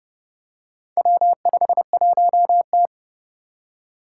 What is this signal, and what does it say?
Morse code, heard as a keyed tone, about 30 wpm, 700 Hz. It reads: W51T
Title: Four Challenging Letter-Number Combinations 30wpm